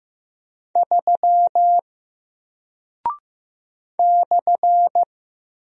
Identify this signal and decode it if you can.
Morse code, heard as a keyed tone, about 15 wpm, 700 Hz. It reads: 3 /
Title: Single Character (Letter:Number:Punctuation:Prosign) 15wpm